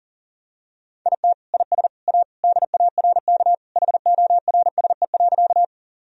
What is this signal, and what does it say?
Morse code, heard as a keyed tone, about 40 wpm, 700 Hz. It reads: IT IS A DARK HORSE.
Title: Sentences from Top 1000 Words - Review Part 1 - 40wpm